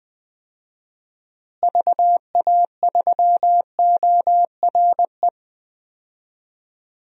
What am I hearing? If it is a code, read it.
Morse code, heard as a keyed tone, about 20 wpm, 700 Hz. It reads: VA3ORE
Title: DXCC-100 Call Signs - 6 Characters - Repeated 2x - 20wpm